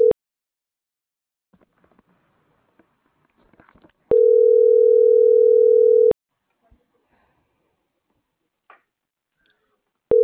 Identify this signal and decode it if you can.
Ringback tone (North America), 440 + 480 Hz.